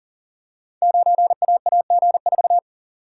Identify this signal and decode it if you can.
Morse code, heard as a keyed tone, about 41 wpm, 700 Hz. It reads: 9AAG4
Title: Code Groups - 5 Letter:Numbers - 40wpm